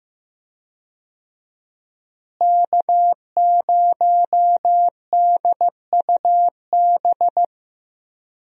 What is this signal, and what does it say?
Morse code, heard as a keyed tone, about 15 wpm, 700 Hz. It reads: K0DUB
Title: USA Call Signs - 5 Characters - Repeated 2x - 15wpm